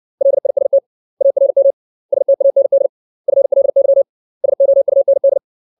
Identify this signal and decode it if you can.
Morse code, heard as a keyed tone, about 60 wpm, 550 Hz. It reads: LIST ARM STAND FLY SOUND
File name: 632 words / 632 Words 60WPM